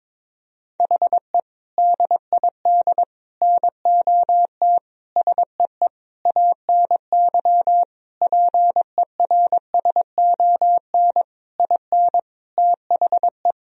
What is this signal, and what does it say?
Morse code, heard as a keyed tone, about 22 wpm, 700 Hz. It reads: HE DID NOT SEE ANY PERSON IN THE
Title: Sentences from Top 1000 Words - Review Part 3 - 22wpm